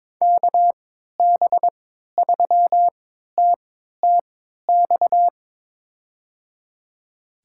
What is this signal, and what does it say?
Morse code, heard as a keyed tone, about 22 wpm, spaced as for 13 wpm, 700 Hz. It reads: KB3TTX
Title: USA Call Signs - 6 Characters - 3x Character Spacing - 22wpm